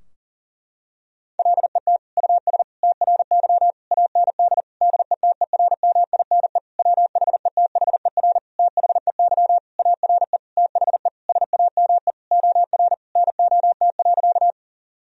Morse code, heard as a keyed tone, about 40 wpm, 700 Hz. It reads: LET US TRY AND DETERMINE WHETHER THEY ARE THE SAME OR NOT.
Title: Sentences from Top 1000 Words - Review Part 4 - 40wpm